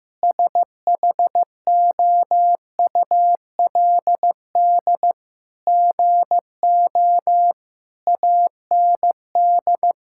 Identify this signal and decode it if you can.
Morse code, heard as a keyed tone, about 15 wpm, 700 Hz. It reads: SHOULD GO AND
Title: Sentences from Top 1600 Words 15wpm